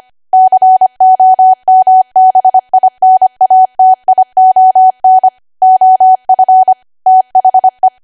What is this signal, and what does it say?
Morse code, heard as a keyed tone, about 25 wpm, 750 Hz. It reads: COMBINATION OF THE
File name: Archive / 25 WPM / 25WPM